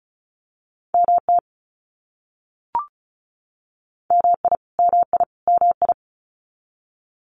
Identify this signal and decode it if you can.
Morse code, heard as a keyed tone, about 34 wpm, 700 Hz. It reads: MT MI MI MI
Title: US States Repeated 3x 35wpm